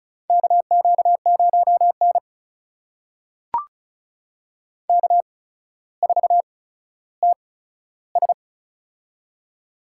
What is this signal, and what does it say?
Morse code, heard as a keyed tone, about 36 wpm, spaced as for 15 wpm, 700 Hz. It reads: KQ0N K4TS
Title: USA Call Signs - 4 Characters - 8x Character Spacing - 35wpm